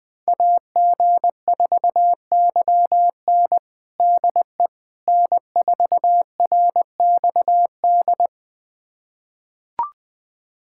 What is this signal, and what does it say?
Morse code, heard as a keyed tone, about 20 wpm, 700 Hz. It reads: AG4YN DE N4RXD
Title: USA Callsign DE Callsign - 5 Characters - 20wpm